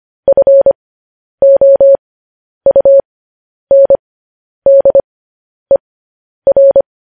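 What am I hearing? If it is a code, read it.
Morse code, heard as a keyed tone, about 25 wpm, spaced as for 10 wpm, 550 Hz. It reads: FOUNDER
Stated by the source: Morse code practice sending news headlines at 10 wpm.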